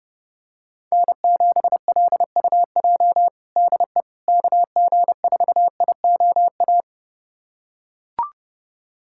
Morse code, heard as a keyed tone, about 28 wpm, 700 Hz. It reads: N7LUJ DE KG4IOA
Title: USA Callsign DE Callsign 30wpm